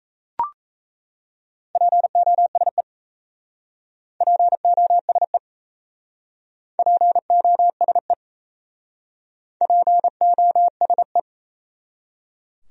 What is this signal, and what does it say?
Morse code, heard as a keyed tone, about 37 wpm, 700 Hz. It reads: POSE POSE POSE POSE